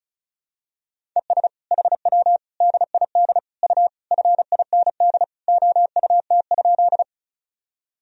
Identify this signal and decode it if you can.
Morse code, heard as a keyed tone, about 36 wpm, 700 Hz. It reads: ES HW DID U FIND OUT?
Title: Sentences from Top 100 Words and Abbreviations 35wpm